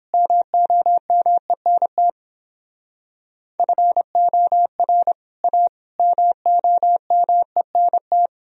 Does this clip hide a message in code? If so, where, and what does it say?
Yes: Morse code from 0.1 to 8.3 s, heard as a keyed tone, about 29 wpm, 700 Hz. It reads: MOMENT FOR A MOMENT